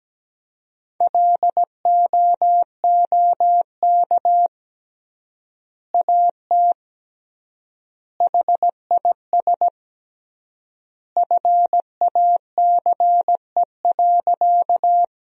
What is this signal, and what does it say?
Morse code, heard as a keyed tone, about 17 wpm, 700 Hz. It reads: LOOK AT HIS FACE.
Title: 3x Word-Spacing - Sentences from Top 300 Words - 17wpm